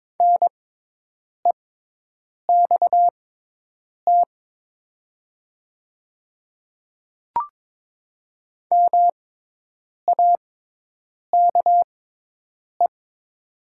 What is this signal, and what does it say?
Morse code, heard as a keyed tone, about 22 wpm, spaced as for 8 wpm, 700 Hz. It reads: NEXT MAKE
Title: 6x Character Spacing - Top 300 Words - 22wpm